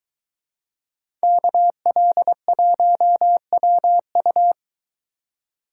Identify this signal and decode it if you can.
Morse code, heard as a keyed tone, about 23 wpm, 700 Hz. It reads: KL1WU